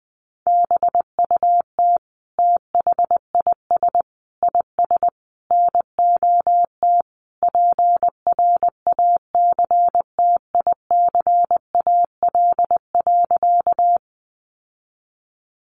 Morse code, heard as a keyed tone, about 19 wpm, 700 Hz. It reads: BUT THIS IS NOT PRACTICAL.
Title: Sentences from Top 1400 Words 20wpm